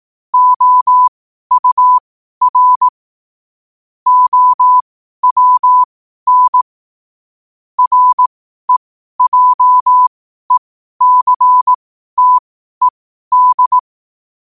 Morse code, heard as a keyed tone, about 18 wpm, spaced as for 12 wpm, 1000 Hz. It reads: OUR OWN REJECTED
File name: qotdfiles / 12wpm